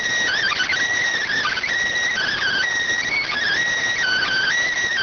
Emergency Alert System (SAME) tones (header bursts), heard as harsh buzzing bursts. It is heard from 0.3 to 4.6 s.